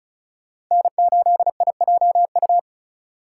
Morse code, heard as a keyed tone, about 45 wpm, 700 Hz. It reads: N8IJU